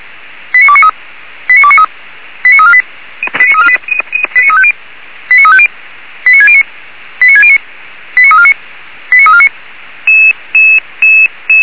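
Five-tone selective call (ZVEI), heard as quick runs of five tones, tones of about 70 ms. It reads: 89293 89293 89348 89348 89350 89350 89609 89609 89349 89349
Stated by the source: ZVEI five-tone selective call 89248 received by SDR.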